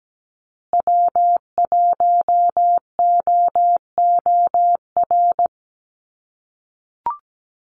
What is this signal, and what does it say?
Morse code, heard as a keyed tone, about 17 wpm, 700 Hz. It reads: W1OOR
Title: USA Call Signs Repeated 2x 17wpm